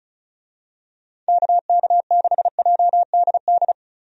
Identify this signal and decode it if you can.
Morse code, heard as a keyed tone, about 36 wpm, 700 Hz. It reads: KK6JDD
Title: USA Call Signs - 6 Characters - 35wpm